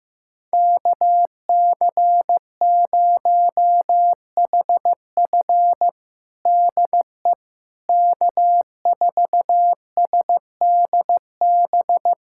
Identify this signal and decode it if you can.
Morse code, heard as a keyed tone, about 15 wpm, 700 Hz. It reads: KC0HF DE K4SDB
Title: USA Callsign DE Callsign - 5 Characters - 15wpm